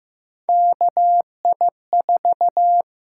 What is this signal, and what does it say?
Morse code, heard as a keyed tone, about 15 wpm, 700 Hz. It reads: KI4
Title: USA Call Signs - 6 Characters - 6x Character Spacing - 15wpm